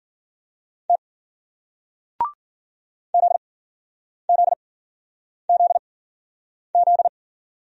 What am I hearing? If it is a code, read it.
Morse code, heard as a keyed tone, about 55 wpm, 700 Hz. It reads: T Z Z Z Z